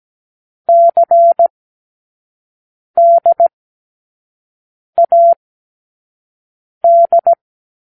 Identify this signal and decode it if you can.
Morse code, heard as a keyed tone, about 17 wpm, 680 Hz. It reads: C D A D
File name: k7qo / Lesson6(A-D)